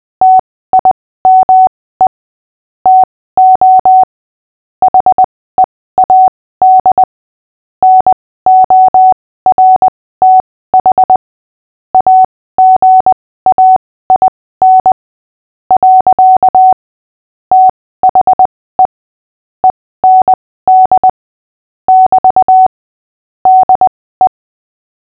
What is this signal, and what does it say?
Morse code, heard as a keyed tone, about 20 wpm, spaced as for 15 wpm, 750 Hz. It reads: TIME TO HEAD NORTH AGAIN. THE END <BT> DE
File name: cw / fists / Short-Story-5-15wpm